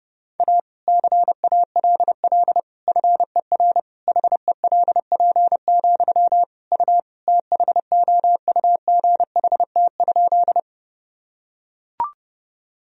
Morse code, heard as a keyed tone, about 30 wpm, 700 Hz. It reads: A CALL FER HELP, U THOUGHT?
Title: Sentences from Top 300 Words and Abbreviations 30wpm